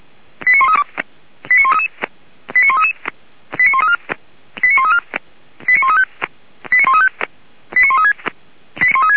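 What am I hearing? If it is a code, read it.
Five-tone selective call (ZVEI), heard as quick runs of five tones, tones of about 80 ms. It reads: 89132 89133 89133 89134 89134 89135 89135 89136
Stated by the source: ZVEI five-tone call 89134 received on an SDR, channel 464.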